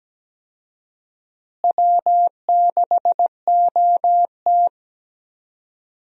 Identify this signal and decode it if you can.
Morse code, heard as a keyed tone, about 17 wpm, 700 Hz. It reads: W6OT
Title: USA Call Signs - 4 Characters - Repeated 3x - 17wpm